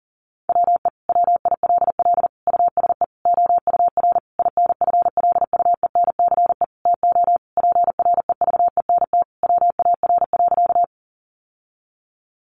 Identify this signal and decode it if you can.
Morse code, heard as a keyed tone, about 40 wpm, 700 Hz. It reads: WE WILL USE OUR INFLUENCE TO PREVENT WAR.
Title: Sentences from Top 1100 Words 40wpm